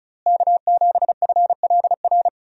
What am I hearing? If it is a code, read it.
Morse code, heard as a keyed tone, about 34 wpm, 700 Hz. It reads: K7FLR
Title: USA Call Signs - 5 Characters - Repeated 3x - 35wpm